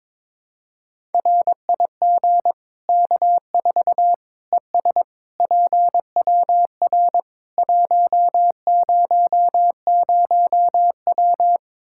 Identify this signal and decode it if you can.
Morse code, heard as a keyed tone, about 22 wpm, 700 Hz. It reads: RIG K4 ES PWR 100W
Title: Practice QSO - Exchange 2 - Info - 22wpm